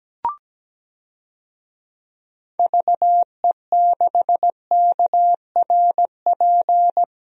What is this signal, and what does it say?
Morse code, heard as a keyed tone, about 17 wpm, 700 Hz. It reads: VE6KRP
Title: DXCC-100 Call Signs - 6 Characters - 17wpm